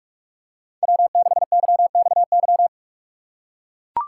Morse code, heard as a keyed tone, about 44 wpm, 700 Hz. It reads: W6YXY
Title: USA Call Signs - 5 Characters - 3x Character Spacing - 45wpm